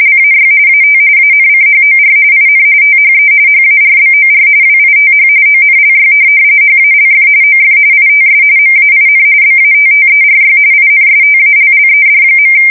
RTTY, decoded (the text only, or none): LAY AT ANCHOR IN THE HARBOR WITH ONLY HALF OF ITS MAIN BATTERY INSTALLED.